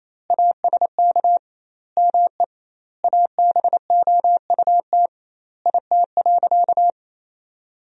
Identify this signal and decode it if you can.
Morse code, heard as a keyed tone, about 28 wpm, 700 Hz. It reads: ASK ME ABOUT IT.
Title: 2x Word-Spacing - Sentences from Top 200 Words - 28wpm